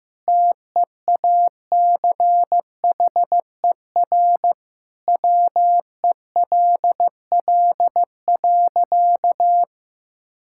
Morse code, heard as a keyed tone, about 14 wpm, 700 Hz. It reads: TEACHER WELL.
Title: Sentences from Top 1000 Words - Review Part 6 - 15wpm